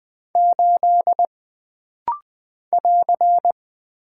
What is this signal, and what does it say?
Morse code, heard as a keyed tone, about 20 wpm, 700 Hz. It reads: 8 <AR>
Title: Single Character (Letter:Number:Punctuation:Prosign) - Rapid-Fire - 20wpm